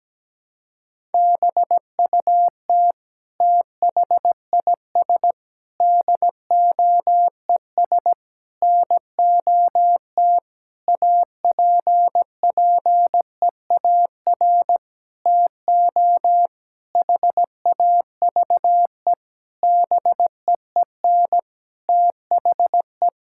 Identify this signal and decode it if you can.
Morse code, heard as a keyed tone, about 17 wpm, 700 Hz. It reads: BUT THIS DOES NOT APPEAR TO HAVE BEEN THE
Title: Sentences from Top 1000 Words - Review Part 6 - 17wpm